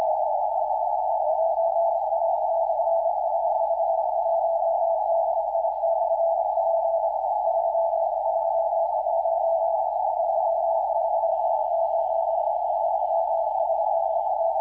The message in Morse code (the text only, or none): none